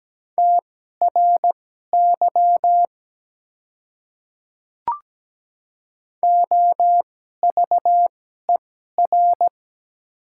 Morse code, heard as a keyed tone, about 17 wpm, spaced as for 12 wpm, 700 Hz. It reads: TRY OVER
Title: 2x Character Spacing - Top 300 Words - 17wpm